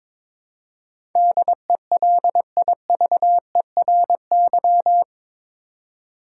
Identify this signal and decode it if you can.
Morse code, heard as a keyed tone, about 22 wpm, 700 Hz. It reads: DELIVERY
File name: practice / 22wpm / Eight Letter Words - 22wpm